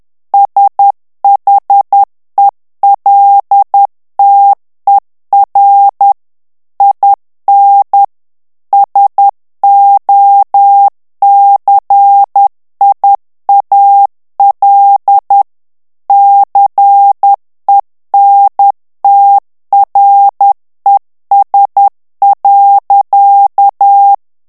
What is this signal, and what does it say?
Morse code, heard as a keyed tone, about 10 wpm, 800 Hz. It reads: SHELTER IN SOCIAL CENTRES.
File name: morse mp3 / ANGLAIS texte clair 2 vitesse 10 mots minute